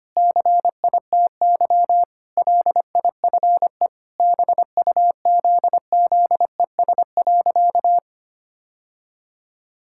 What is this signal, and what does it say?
Morse code, heard as a keyed tone, about 25 wpm, 700 Hz. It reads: CITY LIFE BUZZES.
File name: practice / 25wpm / Sentences with 4-Word Phrases - 25wpm